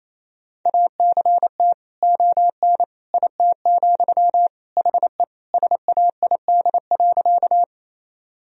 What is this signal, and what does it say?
Morse code, heard as a keyed tone, about 28 wpm, 700 Hz. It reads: ACT ON IT, HE SAID.
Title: Sentences from Top 1000 Words - Review Part 1 - 28wpm